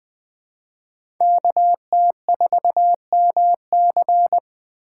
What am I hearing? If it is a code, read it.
Morse code, heard as a keyed tone, about 20 wpm, 700 Hz. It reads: KT4MC
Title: USA Call Signs - 5 Characters - 4x Character Spacing - 20wpm